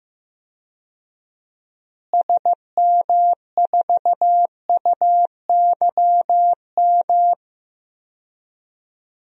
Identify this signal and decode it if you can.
Morse code, heard as a keyed tone, about 15 wpm, 700 Hz. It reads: SM4UYM
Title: DXCC-100 Call Signs - 6 Characters - Repeated 2x - 15wpm